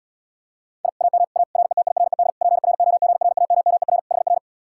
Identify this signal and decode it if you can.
Morse code, heard as a keyed tone, about 79 wpm, 700 Hz. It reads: I AM A RETIRED PROGRAMMER HR